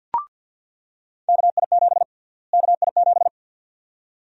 Morse code, heard as a keyed tone, about 47 wpm, 700 Hz. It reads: KI7 KI7
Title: USA Call Sign Prefixes - Repeated 2x - 50wpm